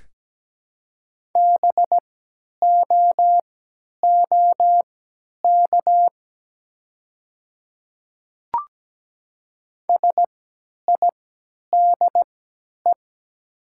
Morse code, heard as a keyed tone, about 17 wpm, spaced as for 10 wpm, 700 Hz. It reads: BOOK SIDE
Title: 3x Character Spacing - Top 300 Words - 17wpm